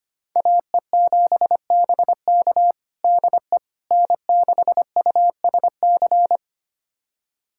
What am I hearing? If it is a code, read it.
Morse code, heard as a keyed tone, about 25 wpm, 700 Hz. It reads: AE7BK DE N6USC
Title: USA Callsign DE Callsign - 5 Characters - 25wpm